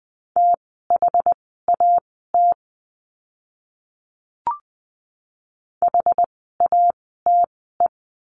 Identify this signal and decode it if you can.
Morse code, heard as a keyed tone, about 20 wpm, spaced as for 15 wpm, 700 Hz. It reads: THAT HATE
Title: 061 Lesson 11f - Letter h - Words 20wpm